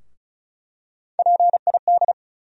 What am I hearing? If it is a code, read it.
Morse code, heard as a keyed tone, about 34 wpm, 700 Hz. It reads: PID
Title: USA Call Sign Postfixes - 3 Letters 35wpm